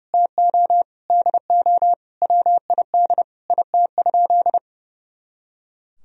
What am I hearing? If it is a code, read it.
Morse code, heard as a keyed tone, about 32 wpm, 700 Hz. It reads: TO DO WID IT?
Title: Sentences from Top 100 Words and Abbreviations 30wpm